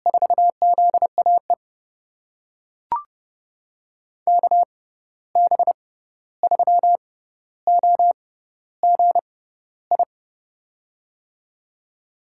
Morse code, heard as a keyed tone, about 30 wpm, spaced as for 14 wpm, 700 Hz. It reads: 4ZAE KB3OGI
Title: USA Call Signs - 6 Characters - 6x Character Spacing - 30wpm